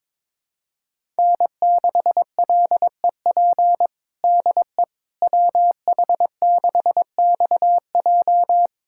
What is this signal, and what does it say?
Morse code, heard as a keyed tone, about 22 wpm, 700 Hz. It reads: N6LEP DE WH6XJ
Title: USA Callsign DE Callsign - 5 Characters - 22wpm